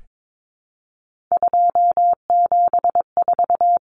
Morse code, heard as a keyed tone, about 22 wpm, 700 Hz. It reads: 274
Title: Numbers - 3-Digit 22wpm